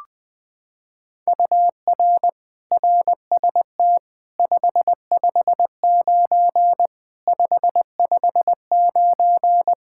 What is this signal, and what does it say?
Morse code, heard as a keyed tone, about 20 wpm, 700 Hz. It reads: UR RST 559 559
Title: Practice QSO - Exchange 1 - Beginning - 20wpm